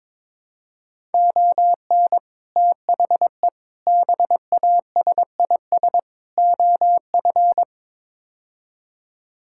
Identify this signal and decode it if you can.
Morse code, heard as a keyed tone, about 22 wpm, 700 Hz. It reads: ON THE BASIS OF
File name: practice / 22wpm / Build A Phrase - 5-Word Phrases - 22wpm